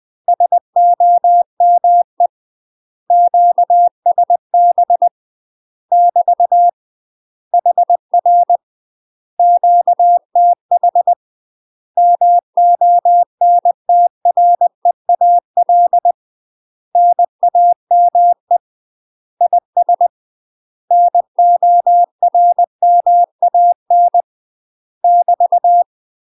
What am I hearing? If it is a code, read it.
Morse code, heard as a keyed tone, about 20 wpm, 700 Hz. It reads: SOME QSB <BT> HR QTH MONTREAL NAME IS NORMAN <BT>